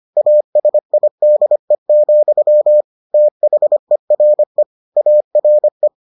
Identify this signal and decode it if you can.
Morse code, heard as a keyed tone, about 25 wpm, 600 Hz. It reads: ASIDE, THERE ARE